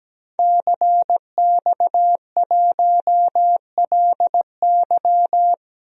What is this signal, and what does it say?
Morse code, heard as a keyed tone, about 17 wpm, 700 Hz. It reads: CX1LY
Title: DXCC-100 Call Signs - 5 Characters - Repeated 2x - 17wpm